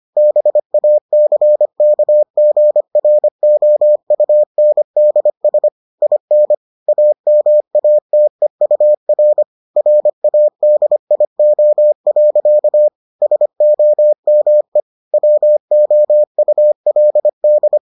Morse code, heard as a keyed tone, about 25 wpm, 600 Hz. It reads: BACKGROUNDS IN AMATEUR RADIO. SOME WOULD